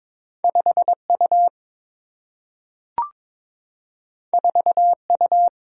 Morse code, heard as a keyed tone, about 22 wpm, 700 Hz. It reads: 5 U 4 U
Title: Single-Letter Single-Number 22wpm